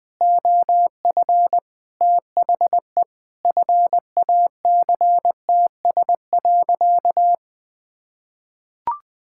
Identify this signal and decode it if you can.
Morse code, heard as a keyed tone, about 20 wpm, 700 Hz. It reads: OF THE FACTS.
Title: Sentences with 5-Word Phrases - Spoken Beginning - 20wpm